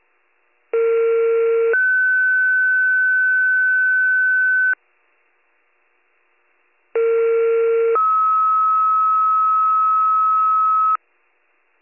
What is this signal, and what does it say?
Two-tone page, heard as a one-second tone, then a three-second tone, about 460 then 1500 Hz, about 460 then 1290 Hz. It is heard from 0.7 to 11.0 s.